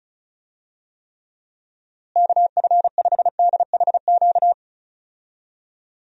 Morse code, heard as a keyed tone, about 36 wpm, 700 Hz. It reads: KF5DHQ
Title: USA Call Signs - 6 Characters - Repeated 3x - 35wpm